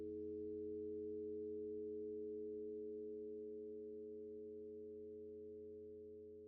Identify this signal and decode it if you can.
Dial tone (Europe, 425 Hz), with noise.